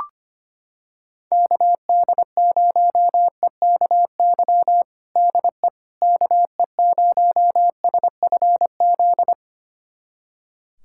Morse code, heard as a keyed tone, about 25 wpm, 700 Hz. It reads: KD0EKY DE KE0SFZ
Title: USA Callsign DE Callsign - 6 Characters - 25wpm